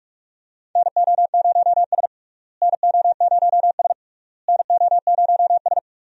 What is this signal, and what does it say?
Morse code, heard as a keyed tone, about 42 wpm, 700 Hz. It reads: NO0S NO0S NO0S
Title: USA Call Signs - 4 Characters - Repeated 3x - 45wpm